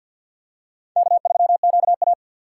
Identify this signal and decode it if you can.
Morse code, heard as a keyed tone, about 54 wpm, 700 Hz. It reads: K3QA
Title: USA Call Signs - 4 Characters - 6x Character Spacing - 50wpm